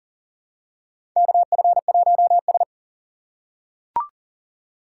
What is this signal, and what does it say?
Morse code, heard as a keyed tone, about 38 wpm, 700 Hz. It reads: KF1S
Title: USA Call Signs - 4 Characters - 6x Character Spacing - 40wpm